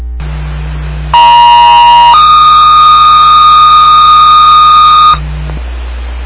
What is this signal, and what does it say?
Two-tone page, heard as a one-second tone, then a three-second tone, about 950 then 1230 Hz, with noise. It is heard from 1.1 to 5.1 s.
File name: Long Lake Fire